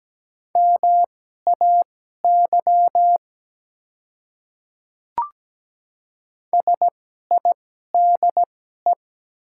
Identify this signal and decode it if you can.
Morse code, heard as a keyed tone, about 17 wpm, spaced as for 12 wpm, 700 Hz. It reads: MAY SIDE
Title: 2x Character Spacing - Top 300 Words - 17wpm